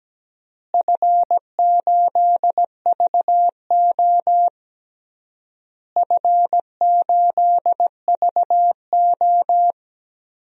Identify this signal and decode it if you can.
Morse code, heard as a keyed tone, about 17 wpm, 700 Hz. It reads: F8VO F8VO
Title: DXCC-100 Call Signs - 4 Characters - Repeated 3x - 17wpm